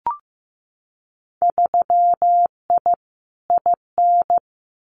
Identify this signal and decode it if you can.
Morse code, heard as a keyed tone, about 15 wpm, 700 Hz. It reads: 3I IN
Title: Winter Field Day Exchanges 15wpm